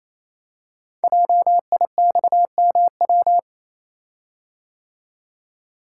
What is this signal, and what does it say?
Morse code, heard as a keyed tone, about 27 wpm, 700 Hz. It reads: JIXMW
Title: Code Groups - 5 Letters - 28wpm